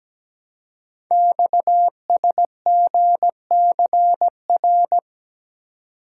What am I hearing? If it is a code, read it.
Morse code, heard as a keyed tone, about 17 wpm, 700 Hz. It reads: XSGCR
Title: Code Groups - 5 Letters - 17wpm